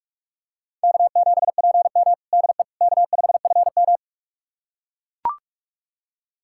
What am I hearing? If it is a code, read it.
Morse code, heard as a keyed tone, about 42 wpm, 690 Hz. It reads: K7PM DE K5FM
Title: USA Callsign DE Callsign - 4 Characters - 45wpm